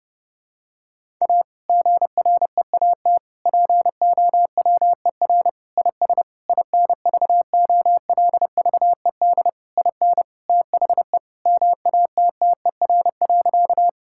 Morse code, heard as a keyed tone, about 32 wpm, 700 Hz. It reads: A GREAT POWER IS INVOLVED IN THE MATTER.
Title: Sentences from Top 1000 Words - Review Part 5 - 30wpm